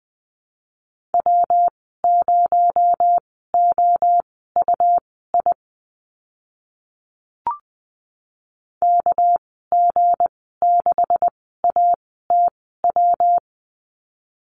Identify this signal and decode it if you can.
Morse code, heard as a keyed tone, about 20 wpm, spaced as for 14 wpm, 700 Hz. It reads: W0OUI KG6ATW
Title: 219 Lesson 35g - Number 0 - Call Signs 20wpm